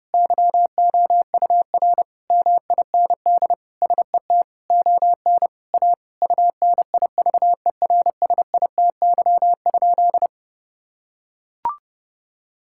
Morse code, heard as a keyed tone, about 28 wpm, 700 Hz. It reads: YOUR MIND SET ON A UNIVERSITY?
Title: Sentences from Top 1000 Words - Review Part 5 - 30wpm